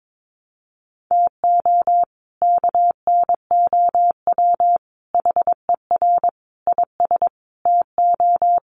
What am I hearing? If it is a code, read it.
Morse code, heard as a keyed tone, about 22 wpm, 700 Hz. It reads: TO KNOW HER IS TO
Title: Sentences from Top 400 Words 22wpm